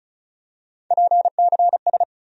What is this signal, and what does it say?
Morse code, heard as a keyed tone, about 36 wpm, 700 Hz. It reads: PCS
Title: USA Call Sign Postfixes - 3 Letters 35wpm